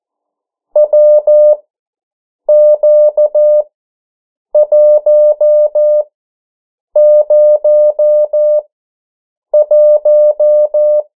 Morse code, heard as a keyed tone, about 14 wpm, spaced as for 7 wpm, 600 Hz. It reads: WQ101
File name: morse mp3 / 09 urok 8